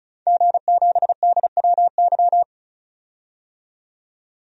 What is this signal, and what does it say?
Morse code, heard as a keyed tone, about 34 wpm, 700 Hz. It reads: G7DWY
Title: DXCC-100 Call Signs - 5 Characters - 35wpm